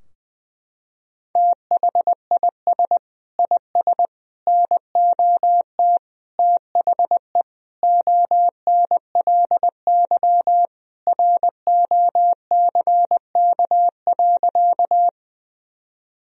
Morse code, heard as a keyed tone, about 20 wpm, 700 Hz. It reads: THIS IS NOT THE ONLY ROCK.
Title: Sentences from Top 1000 Words - Review Part 2 - 20wpm